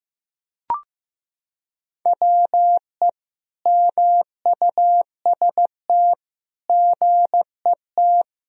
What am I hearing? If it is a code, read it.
Morse code, heard as a keyed tone, about 15 wpm, 700 Hz. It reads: WE MUST GET
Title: Sentences from Top 1000 Words - Review Part 1 - 15wpm